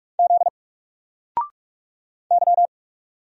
Morse code, heard as a keyed tone, about 44 wpm, 700 Hz. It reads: Z Y
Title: Single Letter-Number 45wpm